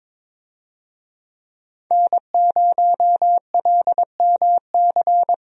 Morse code, heard as a keyed tone, about 22 wpm, 700 Hz. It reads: N0LMC
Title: USA Call Signs - 5 Characters - Repeated 3x - 22wpm